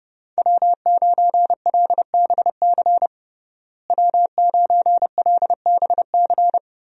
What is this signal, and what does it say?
Morse code, heard as a keyed tone, about 30 wpm, 700 Hz. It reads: W9LBC W9LBC
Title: USA Call Signs - 5 Characters - Repeated 3x - 30wpm